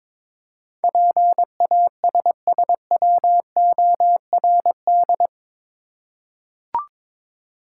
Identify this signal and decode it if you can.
Morse code, heard as a keyed tone, about 22 wpm, 700 Hz. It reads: PASSWORD
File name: practice / 22wpm / Eight Letter Words - 22wpm